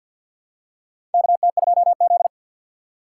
Morse code, heard as a keyed tone, about 50 wpm, 700 Hz. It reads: KT2Z
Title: USA Call Signs - 3x Character Spacing - 50wpm